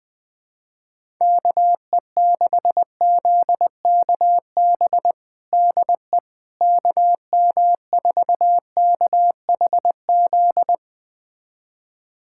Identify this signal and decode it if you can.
Morse code, heard as a keyed tone, about 20 wpm, 700 Hz. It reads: KE6ZKB DE KM4KHZ
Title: USA Callsign DE Callsign 20wpm